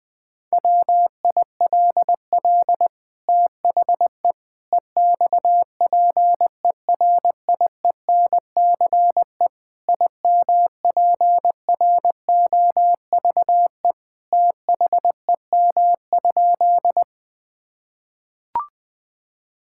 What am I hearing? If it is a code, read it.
Morse code, heard as a keyed tone, about 21 wpm, 700 Hz. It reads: WILL THE EXPERIENCE IMPROVE THEM?
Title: Sentences from Top 1000 Words - Review Part 5 - 20wpm